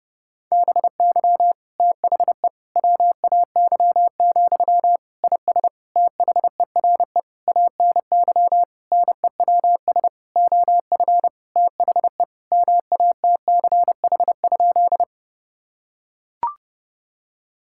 Morse code, heard as a keyed tone, about 32 wpm, 700 Hz. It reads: BY THE WAY, IS THERE ANY NEWS OF THE MATCH?
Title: Sentences from Top 1600 Words 30wpm